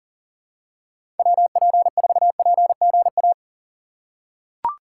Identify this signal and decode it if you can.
Morse code, heard as a keyed tone, about 40 wpm, 700 Hz. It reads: WP4PGA
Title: USA Call Signs - 6 Characters - 2x Character Spacing - 40wpm